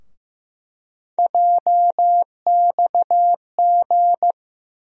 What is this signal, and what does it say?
Morse code, heard as a keyed tone, about 15 wpm, 700 Hz. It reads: JXG
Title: USA Call Sign Postfixes - 3 Letters 15wpm